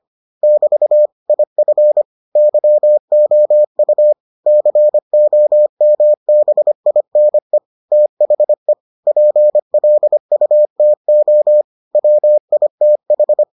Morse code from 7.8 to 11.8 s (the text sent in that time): THE PLUTO